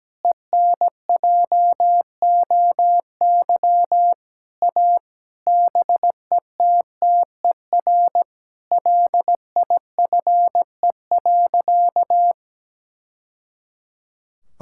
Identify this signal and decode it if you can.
Morse code, heard as a keyed tone, about 17 wpm, 700 Hz. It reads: ENJOY A BETTER LIFE.
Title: Sentences from Top 1000 Words - Review Part 5 - 17wpm